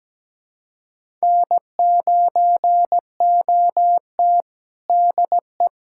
Morse code, heard as a keyed tone, about 17 wpm, 700 Hz. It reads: N9OT DE
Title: USA Callsign DE Callsign - 4 Characters - 17wpm